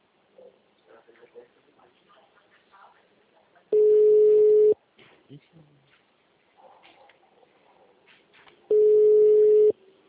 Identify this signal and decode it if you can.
Ringback tone (Europe, 425 Hz), with speech.